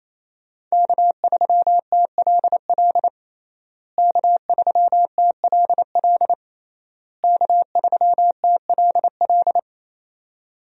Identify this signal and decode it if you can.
Morse code, heard as a keyed tone, about 28 wpm, 700 Hz. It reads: K3TLL K3TLL K3TLL
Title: USA Call Signs - 5 Characters - Repeated 3x - 28wpm